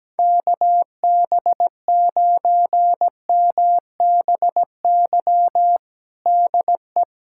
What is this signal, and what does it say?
Morse code, heard as a keyed tone, about 17 wpm, 700 Hz. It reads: KB9MBY DE
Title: USA Callsign DE Callsign - 6 Characters - 17wpm